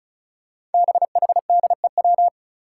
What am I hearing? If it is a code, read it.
Morse code, heard as a keyed tone, about 34 wpm, 700 Hz. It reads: BHDEW
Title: Five Challenging Letter-Number Combinations 35wpm